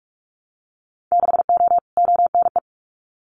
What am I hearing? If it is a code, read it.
Morse code, heard as a keyed tone, about 46 wpm, 700 Hz. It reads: 6O ONE
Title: Winter Field Day Exchanges Encore 45wpm